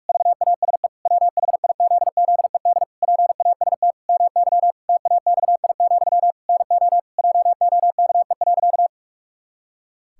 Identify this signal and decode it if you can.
Morse code, heard as a keyed tone, about 41 wpm, 700 Hz. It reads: VASE WHIZZED PAST MY TAXI, NO JOKE.
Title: Pangram Challenge v2 - 45wpm